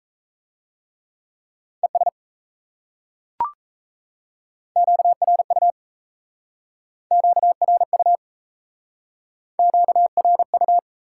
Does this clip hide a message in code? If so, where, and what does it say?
Yes: Morse code from 1.8 to 10.8 s, heard as a keyed tone, about 36 wpm, 700 Hz. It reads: ES QRU QRU QRU